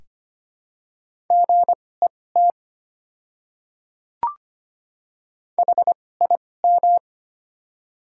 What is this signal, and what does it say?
Morse code, heard as a keyed tone, about 25 wpm, spaced as for 18 wpm, 700 Hz. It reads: GET HIM
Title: 2x Character Spacing - Top 200 Words - 25wpm